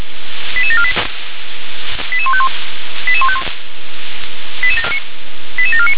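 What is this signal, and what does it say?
Five-tone selective call (ZVEI), heard as quick runs of five tones, tones of about 70 ms. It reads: 88748 88151 88151 88749
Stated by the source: ZVEI five-tone selective call received with an SDR.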